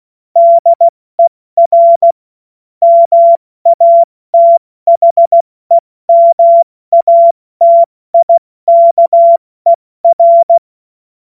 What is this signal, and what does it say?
Morse code, heard as a keyed tone, about 16 wpm, 680 Hz. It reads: DER MATHEMATIKER
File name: Morsekurs / Witze / Witz 10.Oktober 2023 70 F80